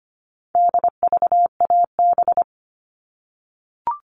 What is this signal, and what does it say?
Morse code, heard as a keyed tone, about 25 wpm, 700 Hz. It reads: DVAB
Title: All Cut Numbers - 4-Digit 25wpm